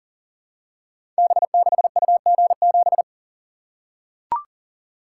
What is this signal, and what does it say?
Morse code, heard as a keyed tone, about 42 wpm, 690 Hz. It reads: B6UG7
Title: Code Groups - 5 Letter:Numbers - 40wpm